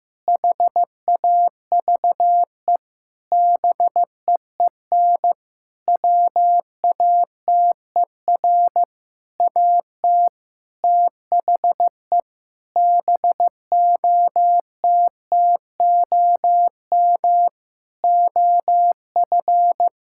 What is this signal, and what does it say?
Morse code, heard as a keyed tone, about 15 wpm, 700 Hz. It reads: HAVE BEEN WATER AT THE BOTTOM OF
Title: Sentences from Top 1000 Words - Review Part 6 - 15wpm